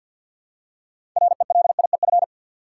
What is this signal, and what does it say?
Morse code, heard as a keyed tone, about 47 wpm, 700 Hz. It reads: RELIEF
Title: Six Letter Words - 50wpm